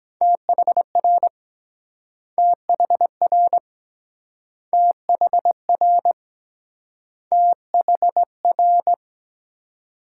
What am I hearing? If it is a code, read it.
Morse code, heard as a keyed tone, about 24 wpm, 700 Hz. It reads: THR THR THR THR